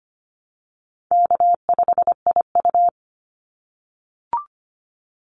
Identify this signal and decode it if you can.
Morse code, heard as a keyed tone, about 25 wpm, 700 Hz. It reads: K5IU
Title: Call Signs - 25wpm